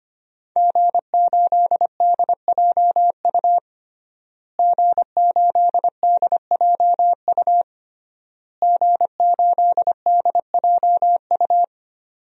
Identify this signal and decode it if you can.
Morse code, heard as a keyed tone, about 25 wpm, 700 Hz. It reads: G8DJU G8DJU G8DJU
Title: DXCC-100 Call Signs - 5 Characters - Repeated 3x - 25wpm